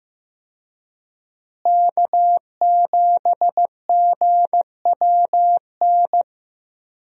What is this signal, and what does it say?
Morse code, heard as a keyed tone, about 16 wpm, 700 Hz. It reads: K7GWN
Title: USA Call Signs - 5 Characters - Repeated 3x - 15wpm